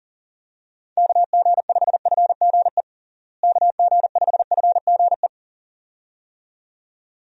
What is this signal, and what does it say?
Morse code, heard as a keyed tone, about 40 wpm, 700 Hz. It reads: KG5FGE KG5FGE
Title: USA Call Signs - 6 Characters - Repeated 2x - 40wpm